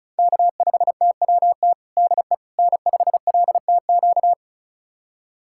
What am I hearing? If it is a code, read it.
Morse code, heard as a keyed tone, about 36 wpm, 700 Hz. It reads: K5TWT DE N5LTQ
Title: USA Callsign DE Callsign - 5 Characters - 35wpm